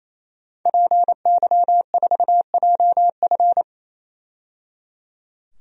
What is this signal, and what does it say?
Morse code, heard as a keyed tone, about 28 wpm, 700 Hz. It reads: PY4JF
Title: DXCC-100 Call Signs - 5 Characters - 28wpm